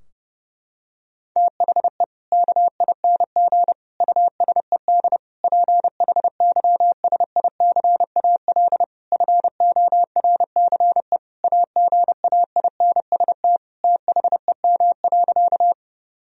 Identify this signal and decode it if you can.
Morse code, heard as a keyed tone, about 32 wpm, 700 Hz. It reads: THE KING USED PHYSICAL FORCE AGAINST THEM.
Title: Sentences from Top 1000 Words - Review Part 4 - 30wpm